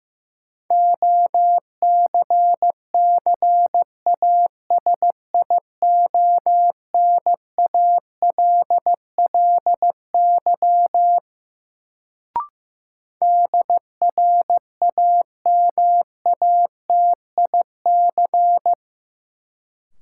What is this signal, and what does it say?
Morse code, heard as a keyed tone, about 15 wpm, 700 Hz. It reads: OCCASIONALLY DRAMATIC
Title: Sentences from Top 1300 Words - New Word Drill - 15wpm